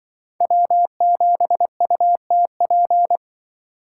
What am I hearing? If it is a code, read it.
Morse code, heard as a keyed tone, about 24 wpm, 700 Hz. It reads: W7UTP
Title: Speed Racing - USA 1x3 Call Signs 20wpm